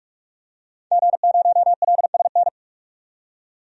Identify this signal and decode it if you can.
Morse code, heard as a keyed tone, about 45 wpm, 700 Hz. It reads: G0LSN
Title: DXCC-100 Call Signs - 5 Characters - 45wpm